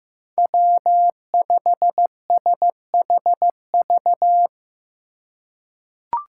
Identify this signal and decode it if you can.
Morse code, heard as a keyed tone, about 15 wpm, 700 Hz. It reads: W5SHV
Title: USA Call Signs - 5 Characters - 3x Character Spacing - 15wpm